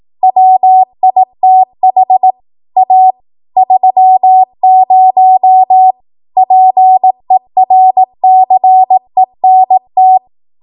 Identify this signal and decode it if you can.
Morse code, heard as a keyed tone, about 18 wpm, 750 Hz. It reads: WITH A 30 PERCENT
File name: Morse / 18WPM